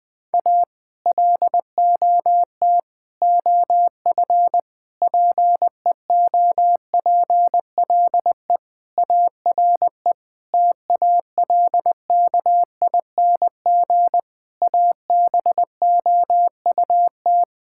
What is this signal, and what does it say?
Morse code, heard as a keyed tone, about 20 wpm, 700 Hz. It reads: A LOT OF PEOPLE ARE TALKING ABOUT
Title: Sentences with 5-Word Phrases - 20wpm